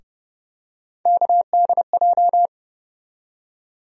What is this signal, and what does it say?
Morse code, heard as a keyed tone, about 30 wpm, 700 Hz. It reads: KDJ
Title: USA Call Sign Postfixes - 3 Letters 30wpm